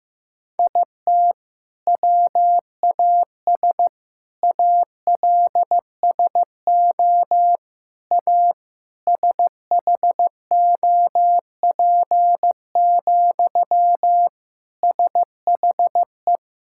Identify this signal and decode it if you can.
Morse code, heard as a keyed tone, about 15 wpm, 700 Hz. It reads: IT WAS ALSO A SHOP, SHE
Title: Sentences from Top 1600 Words 15wpm